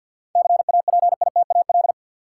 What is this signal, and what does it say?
Morse code, heard as a keyed tone, about 46 wpm, 700 Hz. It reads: CAPITAL
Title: Sentences from Top 1000 Words - New Word Drill - 50wpm